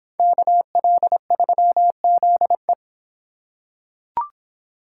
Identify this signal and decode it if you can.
Morse code, heard as a keyed tone, about 25 wpm, 700 Hz. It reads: KL3ZE